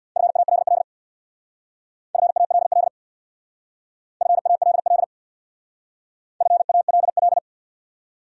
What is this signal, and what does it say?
Morse code, heard as a keyed tone, about 66 wpm, 700 Hz. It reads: FALL FALL FALL FALL